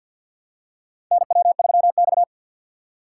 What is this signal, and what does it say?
Morse code, heard as a keyed tone, about 53 wpm, 700 Hz. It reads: NW3X
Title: USA Call Signs - 4x Character Spacing - 50wpm